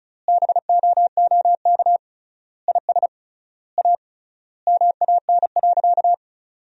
Morse code, heard as a keyed tone, about 36 wpm, 700 Hz. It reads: BOOK IS A MAN.
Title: 3x Word-Spacing - Sentences from Top 500 Words - 35wpm